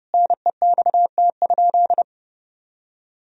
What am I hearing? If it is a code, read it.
Morse code, heard as a keyed tone, about 30 wpm, 700 Hz. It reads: NEXT?
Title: Sentences from Top 1000 Words - Review Part 2 - 30wpm